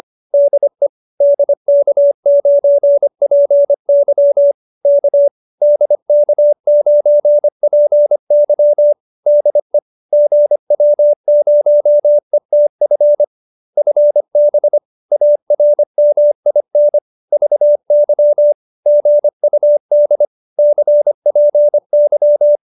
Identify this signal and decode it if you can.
Morse code, heard as a keyed tone, about 25 wpm, 590 Hz. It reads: DE DK9PY K DK9PY DE GW0ETF FB ARMIN VY GUD CPY